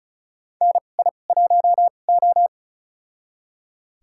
Morse code, heard as a keyed tone, about 36 wpm, spaced as for 25 wpm, 700 Hz. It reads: NI1O
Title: USA Call Signs - 4 Characters - 2x Character Spacing - 35wpm